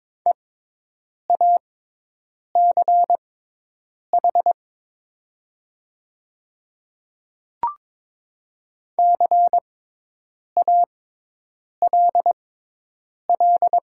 Morse code, heard as a keyed tone, about 22 wpm, spaced as for 8 wpm, 700 Hz. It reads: EACH CALL
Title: 6x Character Spacing - Top 100 Words - 22wpm